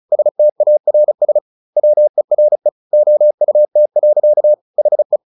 Morse code, heard as a keyed tone, about 35 wpm, 600 Hz. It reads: STARS WERE OUT. HE
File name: the old man and the sea 2 35wpm